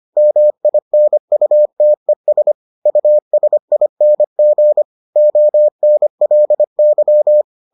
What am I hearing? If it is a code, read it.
Morse code, heard as a keyed tone, about 25 wpm, 600 Hz. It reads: MINUTES USING ONLY